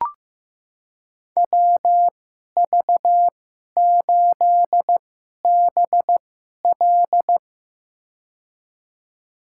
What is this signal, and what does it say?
Morse code, heard as a keyed tone, about 15 wpm, spaced as for 11 wpm, 700 Hz. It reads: WV8BL
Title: USA Call Signs - 5 Characters - 2x Character Spacing - 15wpm